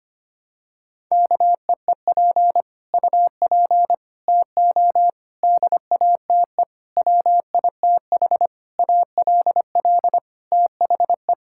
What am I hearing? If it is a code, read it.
Morse code, heard as a keyed tone, about 25 wpm, 700 Hz. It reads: KEEP UP TO DATE WITH ALL THE
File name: Sets of 9 Words - 25wpm